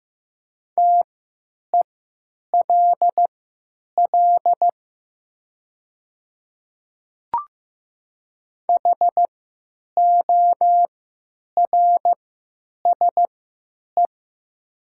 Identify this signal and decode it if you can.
Morse code, heard as a keyed tone, about 15 wpm, spaced as for 9 wpm, 700 Hz. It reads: TELL HORSE
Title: 3x Character Spacing - Top 300 Words - 15wpm